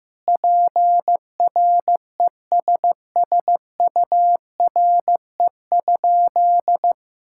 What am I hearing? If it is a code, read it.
Morse code, heard as a keyed tone, about 15 wpm, 700 Hz. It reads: PRESSURE?
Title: Sentences from Top 1000 Words - Review Part 6 - 15wpm